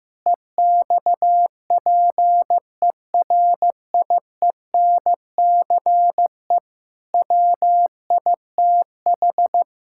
Morse code, heard as a keyed tone, about 16 wpm, 700 Hz. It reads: EXPERIENCE WITH
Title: Sentences from Top 1000 Words - Review Part 5 - 15wpm